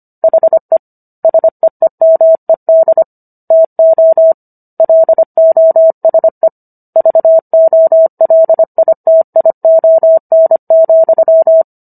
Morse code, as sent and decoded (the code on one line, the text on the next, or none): .... / ... . -- . -.. / - --- / .-.. --- ... / ...- --- .-.. .. - .. --- -. --..--
HE SEEMED TO LOSE VOLITION,